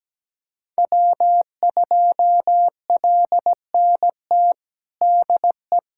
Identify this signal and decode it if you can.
Morse code, heard as a keyed tone, about 17 wpm, 700 Hz. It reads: W2LNT DE
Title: USA Callsign DE Callsign - 5 Characters - 17wpm